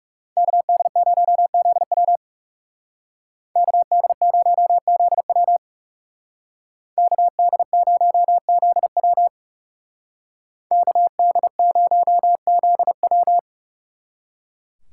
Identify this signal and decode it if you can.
Morse code, heard as a keyed tone, about 39 wpm, 700 Hz. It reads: KD0ZW KD0ZW KD0ZW KD0ZW